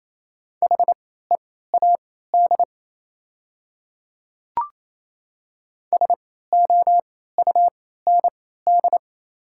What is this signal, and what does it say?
Morse code, heard as a keyed tone, about 28 wpm, spaced as for 16 wpm, 700 Hz. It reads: HEAD SOUND
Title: 3x Character Spacing - Top 300 Words - 28wpm